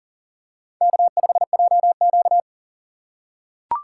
Morse code, heard as a keyed tone, about 38 wpm, 700 Hz. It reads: K5JQ
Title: USA Call Signs - 4 Characters - 3x Character Spacing - 40wpm